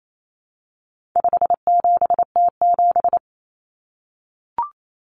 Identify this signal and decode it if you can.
Morse code, heard as a keyed tone, about 28 wpm, 700 Hz. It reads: 57T7
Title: Common Cut Numbers - 4-Digit 28wpm